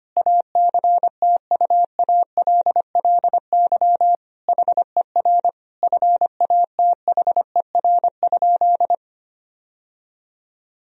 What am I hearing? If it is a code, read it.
Morse code, heard as a keyed tone, about 25 wpm, 700 Hz. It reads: ACTUALLY HER FATHER?
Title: Sentences from Top 1000 Words - Review Part 4 - 25wpm